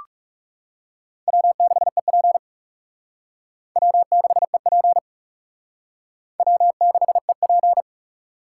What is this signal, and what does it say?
Morse code, heard as a keyed tone, about 36 wpm, 700 Hz. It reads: W6EP W6EP W6EP
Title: Speed Racing - USA 1x2 Call Signs 30wpm